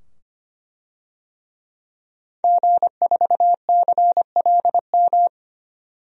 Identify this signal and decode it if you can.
Morse code, heard as a keyed tone, about 25 wpm, 700 Hz. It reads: G4CLM
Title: DXCC-100 Call Signs - 5 Characters - 25wpm